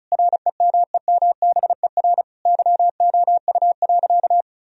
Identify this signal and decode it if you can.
Morse code, heard as a keyed tone, about 34 wpm, 700 Hz. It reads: REMEMBER YOU.
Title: Sentences from Top 1000 Words - Review Part 2 - 35wpm